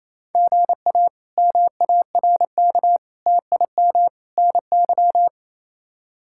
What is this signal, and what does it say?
Morse code, heard as a keyed tone, about 27 wpm, 700 Hz. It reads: GA MARK TIM NY
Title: K1USN SST QSO Practice - 28wpm